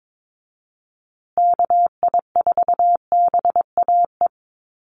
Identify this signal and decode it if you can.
Morse code, heard as a keyed tone, about 22 wpm, 700 Hz. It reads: KI4BAE
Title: USA Call Signs Repeated 2x 22wpm